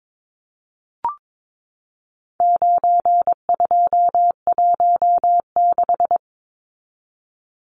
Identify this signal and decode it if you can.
Morse code, heard as a keyed tone, about 22 wpm, 700 Hz. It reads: 9216
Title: Numbers - 4-Digit 22wpm